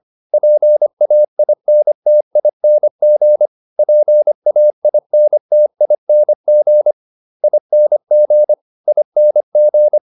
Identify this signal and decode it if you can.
Morse code, heard as a keyed tone, about 25 wpm, 600 Hz. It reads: PAINTING PAINTING ING ING
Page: advanced-practice-files